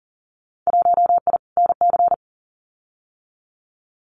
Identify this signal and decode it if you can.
Morse code, heard as a keyed tone, about 40 wpm, 700 Hz. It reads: 1I NC
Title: Winter Field Day Exchanges Encore 40wpm